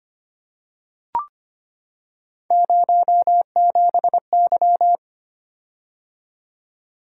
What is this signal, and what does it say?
Morse code, heard as a keyed tone, about 25 wpm, 700 Hz. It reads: 07Y
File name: Three Letter-Number Combinations - 25wpm